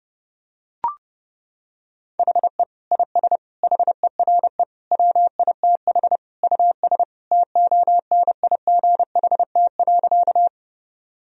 Morse code, heard as a keyed tone, about 30 wpm, 700 Hz. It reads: HE IS HERE WITH US TONIGHT.
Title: Sentences from Top 1000 Words - Review Part 3 - 30wpm